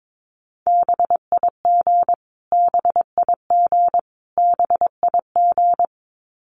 Morse code, heard as a keyed tone, about 22 wpm, 700 Hz. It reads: BIG BIG BIG
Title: Top 500 Words Repeated 22wpm